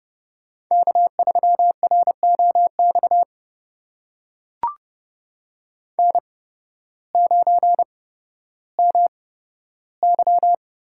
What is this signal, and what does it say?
Morse code, heard as a keyed tone, about 30 wpm, spaced as for 14 wpm, 700 Hz. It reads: K3ROX N9MY
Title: USA Call Signs - 5 Characters - 8x Character Spacing - 30wpm